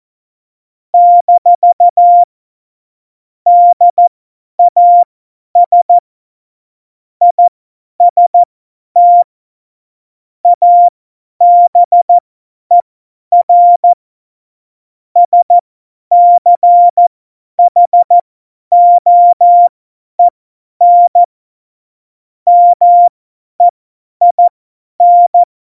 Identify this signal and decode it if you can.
Morse code, heard as a keyed tone, about 14 wpm, spaced as for 10 wpm, 700 Hz. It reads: - DAS IST ABER SCHOEN MEIN
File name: Morsekurs / Witze / Witz 21. November 2017, 50 F70